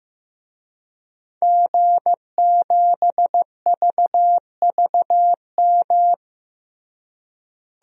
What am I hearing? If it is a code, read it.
Morse code, heard as a keyed tone, about 15 wpm, 700 Hz. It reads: G7VVM
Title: DXCC-100 Call Signs - 5 Characters - Repeated 3x - 15wpm